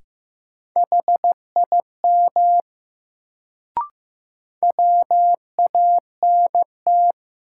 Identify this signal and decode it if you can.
Morse code, heard as a keyed tone, about 15 wpm, 700 Hz. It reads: HIM WANT
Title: Sets of 2 Words - New Word Drill - 15wpm